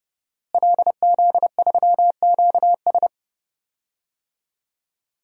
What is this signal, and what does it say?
Morse code, heard as a keyed tone, about 30 wpm, 700 Hz. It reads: LZ3QS
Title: DXCC-100 Call Signs - 5 Characters - Repeated 3x - 30wpm